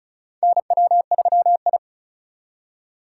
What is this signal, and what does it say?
Morse code, heard as a keyed tone, about 36 wpm, 700 Hz. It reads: NW3I
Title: USA Call Signs - 4 Characters - 4x Character Spacing - 35wpm